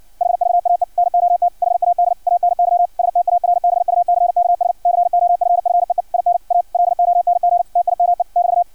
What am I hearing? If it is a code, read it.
Morse code, heard as a keyed tone, about 56 wpm, 700 Hz, with radio noise. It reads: HVAE NOT HAD ANY STARBUCKS COFFEE IN A LONG TIME <BT>
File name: archived / CRSnet